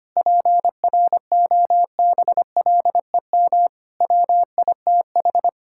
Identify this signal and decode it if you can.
Morse code, heard as a keyed tone, about 25 wpm, 700 Hz. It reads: PROBLEM WITH
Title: Sentences from Top 1000 Words - Review Part 6 - 25wpm